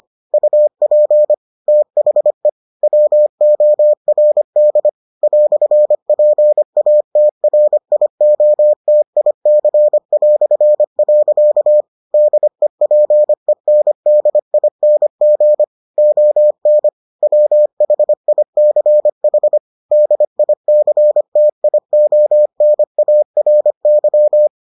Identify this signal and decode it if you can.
Morse code, heard as a keyed tone, about 25 wpm, 600 Hz. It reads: UP THE WORD "PATRIOTIC". DEPENDING ON WHICH DICTIONARY